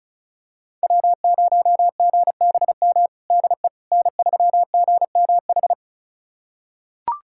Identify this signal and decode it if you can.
Morse code, heard as a keyed tone, about 36 wpm, 700 Hz. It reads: W0GBM DE N3GMH
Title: USA Callsign DE Callsign - 5 Characters - 35wpm